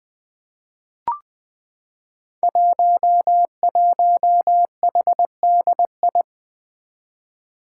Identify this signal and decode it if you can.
Morse code, heard as a keyed tone, about 20 wpm, 700 Hz. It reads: 11HDI
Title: Five Challenging Letter-Number Combinations 20wpm